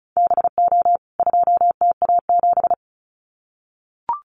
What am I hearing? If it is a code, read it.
Morse code, heard as a keyed tone, about 34 wpm, 700 Hz. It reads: BO 2TA7
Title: CWT Exchange 35wpm